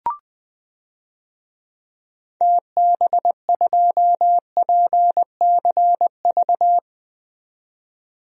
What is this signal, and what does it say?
Morse code, heard as a keyed tone, about 21 wpm, 700 Hz. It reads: TB2PCV
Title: DXCC-100 Call Signs - 6 Characters - 20wpm